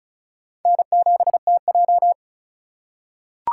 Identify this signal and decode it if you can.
Morse code, heard as a keyed tone, about 36 wpm, 700 Hz. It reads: N7TJ
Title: USA Call Signs - 3x Character Spacing - 35wpm